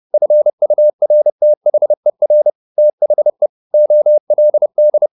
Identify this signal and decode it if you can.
Morse code, heard as a keyed tone, about 30 wpm, 600 Hz. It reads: FURTHER THE OLD
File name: the old man and the sea / the old man and the sea 1 30wpm